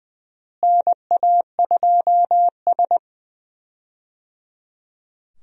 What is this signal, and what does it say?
Morse code, heard as a keyed tone, about 20 wpm, 700 Hz. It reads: NA2S
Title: USA Call Signs - 4 Characters - Repeated 3x - 20wpm